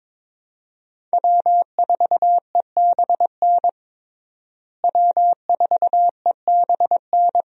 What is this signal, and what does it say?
Morse code, heard as a keyed tone, about 22 wpm, 700 Hz. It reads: W4EBN W4EBN
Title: USA Call Signs - 5 Characters - Repeated 2x - 22wpm